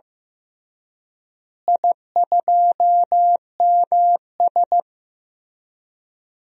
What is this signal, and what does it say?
Morse code, heard as a keyed tone, about 15 wpm, 700 Hz. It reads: I2MS
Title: DXCC-100 Call Signs - 4 Characters - Repeated 3x - 15wpm